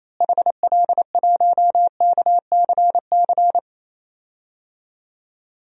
Morse code, heard as a keyed tone, about 28 wpm, 700 Hz. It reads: HL1KCC
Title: DXCC-100 Call Signs - 6 Characters - Repeated 3x - 28wpm